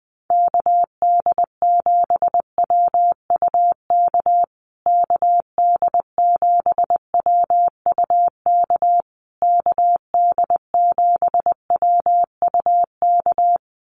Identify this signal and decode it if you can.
Morse code, heard as a keyed tone, about 20 wpm, 700 Hz. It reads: KD7WUK KD7WUK KD7WUK
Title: USA Call Signs Repeated 3x 20wpm